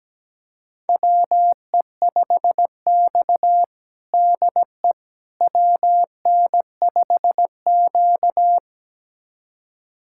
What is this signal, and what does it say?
Morse code, heard as a keyed tone, about 17 wpm, 700 Hz. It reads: WE5X DE WN5Q
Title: USA Callsign DE Callsign - 4 Characters - 17wpm